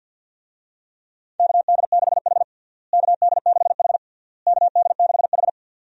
Morse code, heard as a keyed tone, about 50 wpm, 700 Hz. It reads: KD6H KD6H KD6H
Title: USA Call Signs - 4 Characters - Repeated 3x - 50wpm